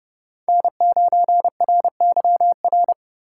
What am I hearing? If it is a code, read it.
Morse code, heard as a keyed tone, about 32 wpm, 700 Hz. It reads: N9RYR
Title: USA Call Signs - 3x Character Spacing - 30wpm